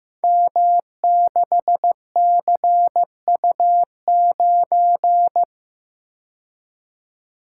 Morse code, heard as a keyed tone, about 15 wpm, 700 Hz. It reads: M6CU9
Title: Code Groups - 5 Letter:Numbers - 15wpm